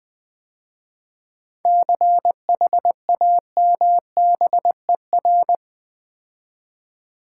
Morse code, heard as a keyed tone, about 20 wpm, 700 Hz. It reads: CHAMBER
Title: 7 Letter Words - First 3 Letters Given - 20wpm